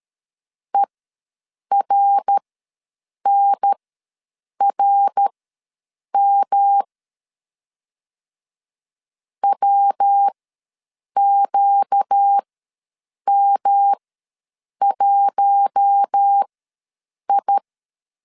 Morse code, heard as a keyed tone, about 13 wpm, spaced as for 7 wpm, 800 Hz. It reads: ERNRM WQM1I
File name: LEZIONI / 5 LBVC / 5 L ws 12wpm 2958